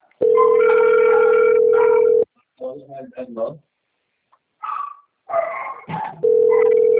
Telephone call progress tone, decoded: ringback tone

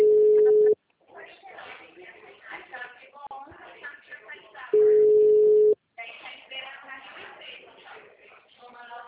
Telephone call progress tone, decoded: ringback tone

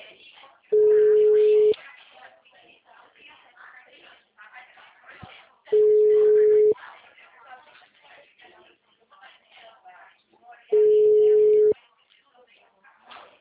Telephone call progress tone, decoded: ringback tone

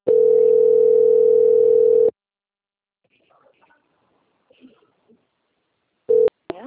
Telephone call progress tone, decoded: ringback tone

